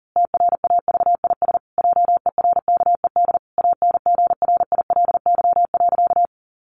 Morse code, heard as a keyed tone, about 40 wpm, 700 Hz. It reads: TRAVIS JERKED ANGRILY.
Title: A Sound Of Thunder 40wpm